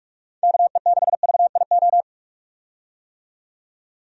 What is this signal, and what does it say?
Morse code, heard as a keyed tone, about 45 wpm, 700 Hz. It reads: KE6VIO
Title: USA Call Signs - 6 Characters - 45wpm